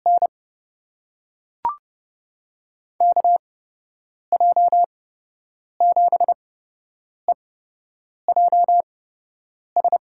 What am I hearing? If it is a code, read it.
Morse code, heard as a keyed tone, about 30 wpm, spaced as for 8 wpm, 700 Hz. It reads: N KJ7EJS
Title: USA Call Signs - 6 Characters - 8x Character Spacing - 30wpm